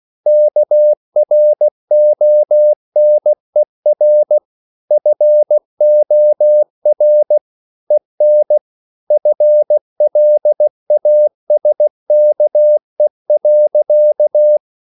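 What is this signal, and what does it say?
Morse code, heard as a keyed tone, about 16 wpm, 600 Hz. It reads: KRONER FOR EN FLASKE.
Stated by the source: Slow (QRS) Morse code at 16 wpm, from CW med Gnister.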